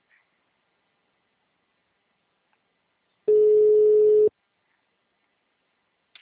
Ringback tone (Europe, 425 Hz).